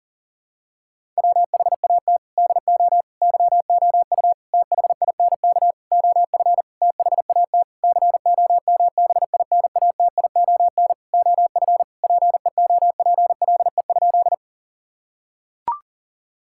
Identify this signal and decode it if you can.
Morse code, heard as a keyed tone, about 40 wpm, 700 Hz. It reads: WHAT DO YOU THINK OF THAT COMBINATION OF PEOPLE?
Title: Sentences from Top 1000 Words - Review Part 6 - 40wpm